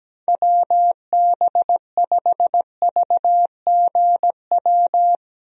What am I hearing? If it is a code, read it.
Morse code, heard as a keyed tone, about 17 wpm, 700 Hz. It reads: WB5VGW
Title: USA Call Signs - 6 Characters - Repeated 3x - 17wpm